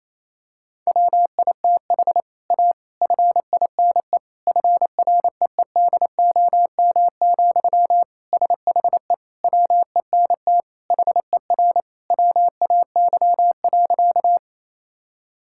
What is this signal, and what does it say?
Morse code, heard as a keyed tone, about 28 wpm, 700 Hz. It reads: WITH A FINE FREEDOM, SHE WENT HER WAY.
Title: Sentences from Top 1000 Words - Review Part 5 - 28wpm